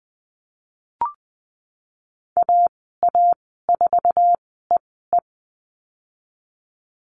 Morse code, heard as a keyed tone, about 20 wpm, spaced as for 15 wpm, 700 Hz. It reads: AA4EE
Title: 048 Lesson 9g - Number 4 - Call Signs 20wpm